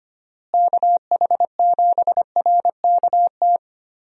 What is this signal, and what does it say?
Morse code, heard as a keyed tone, about 25 wpm, 700 Hz. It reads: KH7RKT
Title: USA Call Signs - 6 Characters - Repeated 3x - 25wpm